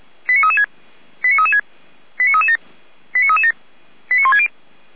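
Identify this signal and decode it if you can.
Five-tone selective call (ZVEI), heard as quick runs of five tones, tones of about 70 ms. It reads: 89336 89336 89337 89337 89160
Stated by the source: Five-tone ZVEI selective call received on an SDR.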